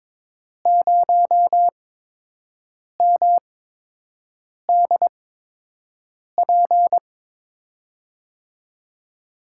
Morse code, heard as a keyed tone, about 22 wpm, spaced as for 6 wpm, 700 Hz. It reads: 0MDP
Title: USA Call Signs - 5 Characters - 8x Character Spacing - 22wpm